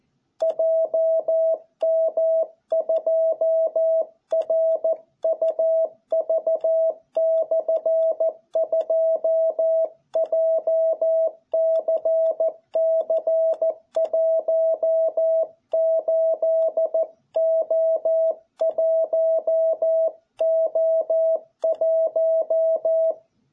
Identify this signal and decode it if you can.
Morse code, heard as a keyed tone, about 14 wpm, 660 Hz. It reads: JM2RUV/2 JCC18O1O1